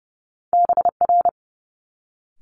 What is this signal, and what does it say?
Morse code, heard as a keyed tone, about 30 wpm, 700 Hz. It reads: BR
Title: Multi-letter Phonemes Quick Fire 30wpm